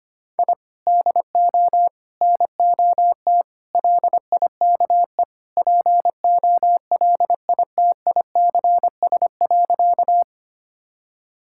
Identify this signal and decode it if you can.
Morse code, heard as a keyed tone, about 25 wpm, 700 Hz. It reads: I DO NOT LIKE POLITICS.
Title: Sentences from Top 1000 Words - Review Part 6 - 25wpm